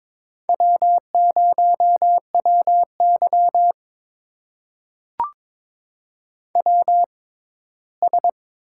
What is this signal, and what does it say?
Morse code, heard as a keyed tone, about 22 wpm, spaced as for 15 wpm, 700 Hz. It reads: W0WY WS
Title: USA Call Signs - 4 Characters - 6x Character Spacing - 22wpm